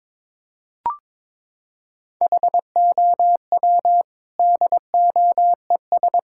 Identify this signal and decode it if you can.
Morse code, heard as a keyed tone, about 22 wpm, 700 Hz. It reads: HOW DOES
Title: Sentences from Top 1600 Words 22wpm